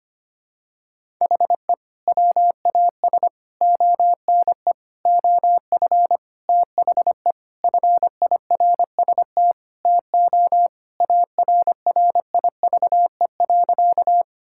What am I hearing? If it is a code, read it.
Morse code, heard as a keyed tone, about 25 wpm, 700 Hz. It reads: HE WAS ONE OF THE FIRST TO ARRIVE.
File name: practice / 25wpm / Sentences with 3-Word Phrases - 25wpm